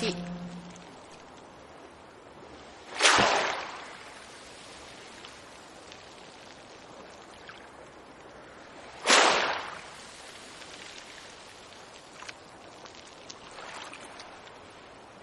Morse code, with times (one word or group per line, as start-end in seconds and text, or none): none